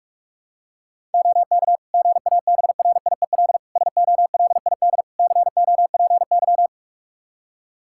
Morse code, heard as a keyed tone, about 41 wpm, 700 Hz. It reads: OK GABRIEL SOLID COPY
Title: Practice QSO - Exchange 3 - More Info - 45wpm